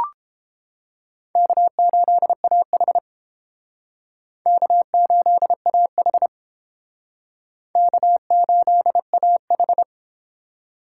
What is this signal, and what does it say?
Morse code, heard as a keyed tone, about 30 wpm, 700 Hz. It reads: K8AH K8AH K8AH